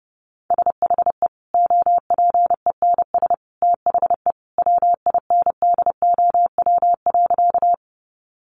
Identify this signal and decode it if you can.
Morse code, heard as a keyed tone, about 30 wpm, 700 Hz. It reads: SHE OPENS THE WINDOW.
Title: Easy Phrases 30wpm